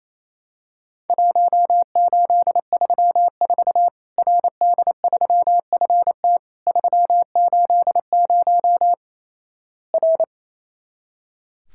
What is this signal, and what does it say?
Morse code, heard as a keyed tone, about 28 wpm, 700 Hz. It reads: 1834 RD3FT 380 R
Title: Worked All Europe DX Contest - QTC Exchange - 28wpm